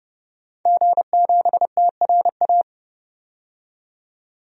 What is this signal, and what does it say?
Morse code, heard as a keyed tone, about 30 wpm, 700 Hz. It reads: G7TRA
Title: DXCC-100 Call Signs - 5 Characters - 30wpm